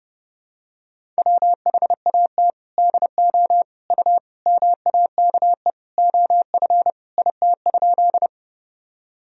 Morse code, heard as a keyed tone, about 32 wpm, 700 Hz. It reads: WHAT DO U MAKE OF IT?
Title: Sentences from Top 100 Words and Abbreviations 30wpm